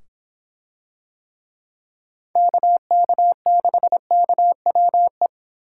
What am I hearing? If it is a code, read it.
Morse code, heard as a keyed tone, about 25 wpm, 700 Hz. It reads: KK6KWE